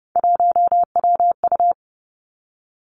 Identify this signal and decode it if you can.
Morse code, heard as a keyed tone, about 30 wpm, 700 Hz. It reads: 1WU
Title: Three Challenging Letter-Number Combinations 30wpm